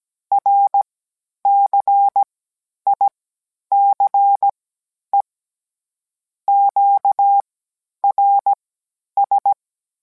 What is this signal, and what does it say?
Morse code, heard as a keyed tone, about 17 wpm, spaced as for 10 wpm, 800 Hz. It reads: RCICE QRS